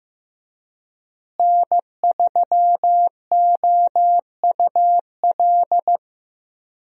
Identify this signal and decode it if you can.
Morse code, heard as a keyed tone, about 15 wpm, 700 Hz. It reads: N3OUL
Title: USA Call Signs - 5 Characters - Repeated 3x - 15wpm